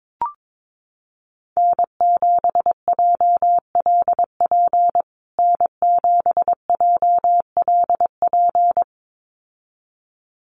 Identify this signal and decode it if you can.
Morse code, heard as a keyed tone, about 22 wpm, 700 Hz. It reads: N7JLP N7JLP
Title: USA Call Signs Repeated 2x 22wpm